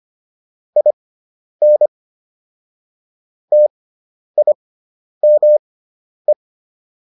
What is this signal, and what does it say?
Morse code, heard as a keyed tone, about 25 wpm, spaced as for 10 wpm, 600 Hz. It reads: IN TIME